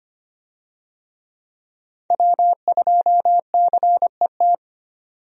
Morse code, heard as a keyed tone, about 25 wpm, 700 Hz. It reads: W2CET
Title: USA Call Signs - 5 Characters - 25wpm